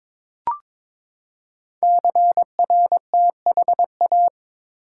Morse code, heard as a keyed tone, about 22 wpm, 700 Hz. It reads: CRTHA
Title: Code Groups - 5 Letters - 22wpm